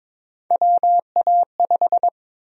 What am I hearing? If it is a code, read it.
Morse code, heard as a keyed tone, about 22 wpm, 700 Hz. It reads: WA5
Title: USA Call Sign Prefixes - 22wpm